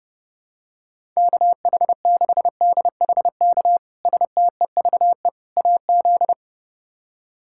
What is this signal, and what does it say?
Morse code, heard as a keyed tone, about 29 wpm, 700 Hz. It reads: KH6DHK STEVE AZ
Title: K1USN SST QSO Practice - 30wpm